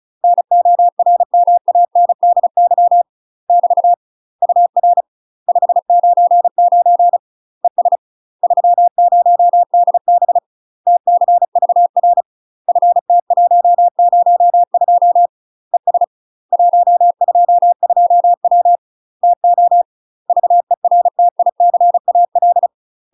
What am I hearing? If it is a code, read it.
Morse code, heard as a keyed tone, about 36 wpm, 700 Hz. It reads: NORMANDY <BT> UR 599 ES 30DB TCVR FT102 ES 122W TO VERTICAL
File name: UK MASTER QSOs TRAINING / 35WPM / UK MASTER QSOs 0018